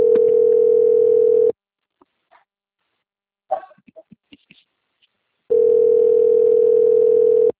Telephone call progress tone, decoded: ringback tone

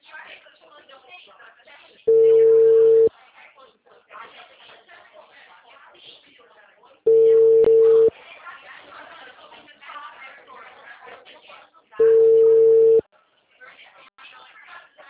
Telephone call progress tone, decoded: ringback tone

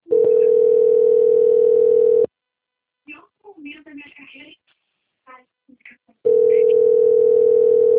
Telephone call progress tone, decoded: ringback tone

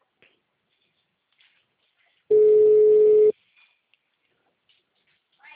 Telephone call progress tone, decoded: ringback tone

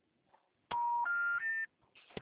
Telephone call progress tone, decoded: special information tone

